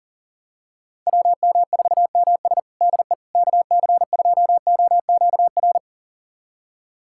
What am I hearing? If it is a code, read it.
Morse code, heard as a keyed tone, about 40 wpm, 700 Hz. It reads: WM4MS DE KC2OQR
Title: USA Callsign DE Callsign 40wpm